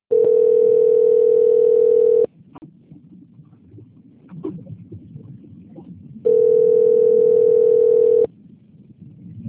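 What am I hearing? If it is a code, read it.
Ringback tone (North America), 440 + 480 Hz.
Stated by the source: Outbound phone call recorded on a PBX at Santa Barbara - SP.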